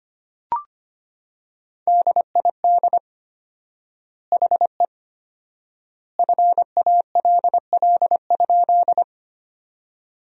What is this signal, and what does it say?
Morse code, heard as a keyed tone, about 25 wpm, 700 Hz. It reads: DID HE FALL?
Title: 4x Word-Spacing - Sentences from Top 500 Words - 25wpm